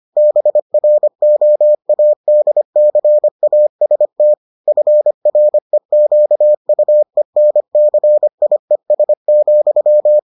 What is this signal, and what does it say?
Morse code, heard as a keyed tone, about 25 wpm, 600 Hz. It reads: BROADCAST FREQUENCIES,